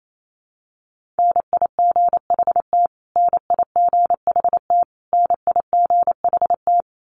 Morse code, heard as a keyed tone, about 28 wpm, 700 Hz. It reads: NIGHT NIGHT NIGHT
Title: Top 500 Words Repeated 28wpm